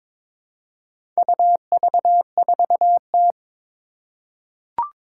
Morse code, heard as a keyed tone, about 22 wpm, 700 Hz. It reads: UV4T
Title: Four Challenging Letter-Number Combinations 22wpm